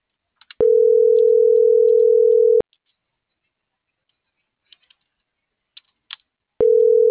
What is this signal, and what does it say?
Ringback tone (North America), 440 + 480 Hz.